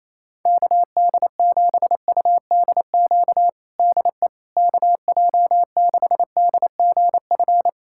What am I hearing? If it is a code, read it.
Morse code, heard as a keyed tone, about 28 wpm, 700 Hz. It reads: KD7UDQ DE KJ6DGF
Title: USA Callsign DE Callsign - 6 Characters - 28wpm